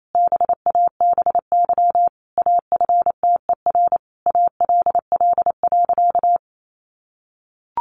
Morse code, heard as a keyed tone, about 28 wpm, 700 Hz. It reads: BABY AFTER ALL.
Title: Sentences from Top 1300 Words 28wpm